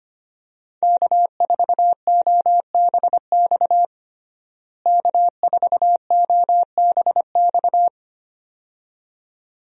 Morse code, heard as a keyed tone, about 25 wpm, 700 Hz. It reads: K4OBX K4OBX
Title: USA Call Signs - 5 Characters - Repeated 2x - 25wpm